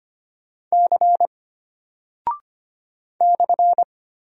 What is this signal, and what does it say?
Morse code, heard as a keyed tone, about 25 wpm, 700 Hz. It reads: C /
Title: Single Character (Letter:Number:Punctuation:Prosign) Alt 25wpm